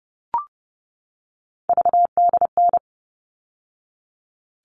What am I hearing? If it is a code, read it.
Morse code, heard as a keyed tone, about 30 wpm, 700 Hz. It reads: VDN
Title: All Cut Numbers - 3-Digit 30wpm